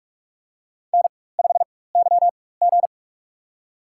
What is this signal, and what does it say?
Morse code, heard as a keyed tone, about 45 wpm, spaced as for 21 wpm, 700 Hz. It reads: N5YG
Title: USA Call Signs - 4 Characters - 4x Character Spacing - 45wpm